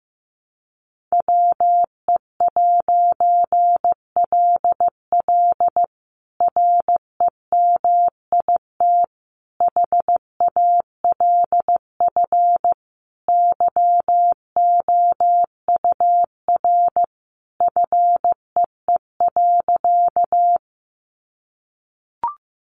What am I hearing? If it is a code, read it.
Morse code, heard as a keyed tone, about 15 wpm, 700 Hz. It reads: WE'LL REMIT HALF YOUR FEE.
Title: A Sound Of Thunder 15wpm